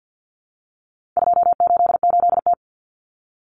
Morse code, heard as a keyed tone, about 50 wpm, 700 Hz. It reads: 288T
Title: Common Cut Numbers - 4-Digit 50wpm